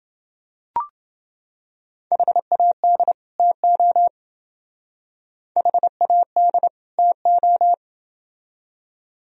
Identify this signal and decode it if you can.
Morse code, heard as a keyed tone, about 28 wpm, 700 Hz. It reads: HAD TO HAD TO